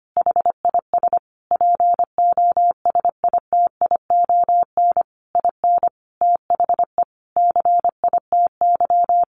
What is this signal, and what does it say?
Morse code, heard as a keyed tone, about 25 wpm, 700 Hz. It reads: HIS POSITION IN THE CITY
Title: Sentences from Top 900 Words 25wpm